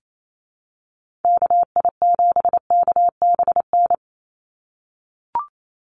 Morse code, heard as a keyed tone, about 28 wpm, 700 Hz. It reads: KI7KBN
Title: USA Call Signs 28wpm